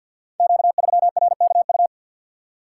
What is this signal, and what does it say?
Morse code, heard as a keyed tone, about 51 wpm, 700 Hz. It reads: Q3RKU
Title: Code Groups - 5 Letter:Numbers - 50wpm